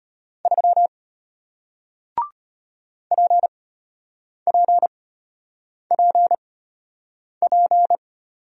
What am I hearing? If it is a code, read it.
Morse code, heard as a keyed tone, about 33 wpm, spaced as for 8 wpm, 700 Hz. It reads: 3 PPPP